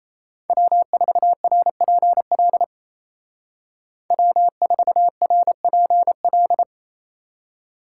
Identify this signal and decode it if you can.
Morse code, heard as a keyed tone, about 34 wpm, 700 Hz. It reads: W4RPL W4RPL
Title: Speed Racing - USA 1x3 Call Signs 28wpm